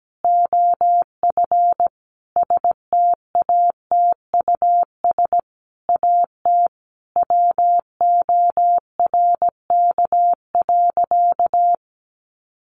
Morse code, heard as a keyed tone, about 17 wpm, 700 Hz. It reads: OF STATUS AT WORK.
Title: Sentences from Top 1400 Words 17wpm